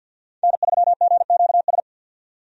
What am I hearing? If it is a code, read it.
Morse code, heard as a keyed tone, about 51 wpm, 700 Hz. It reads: N3GQS
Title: USA Call Signs - 5 Characters - 50wpm